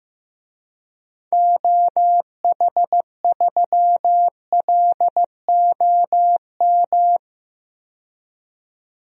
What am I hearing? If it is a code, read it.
Morse code, heard as a keyed tone, about 15 wpm, 700 Hz. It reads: OH3LOM
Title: DXCC-100 Call Signs - 6 Characters - 15wpm